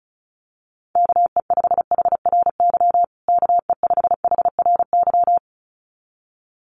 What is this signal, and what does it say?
Morse code, heard as a keyed tone, about 34 wpm, 700 Hz. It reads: KE5HRY KE5HRY
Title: USA Call Signs Repeated 2x 35wpm